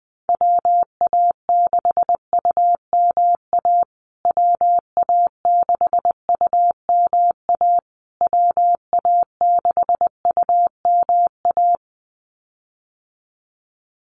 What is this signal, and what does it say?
Morse code, heard as a keyed tone, about 20 wpm, 700 Hz. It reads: WA6UMA WA6UMA WA6UMA
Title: USA Call Signs Repeated 3x 20wpm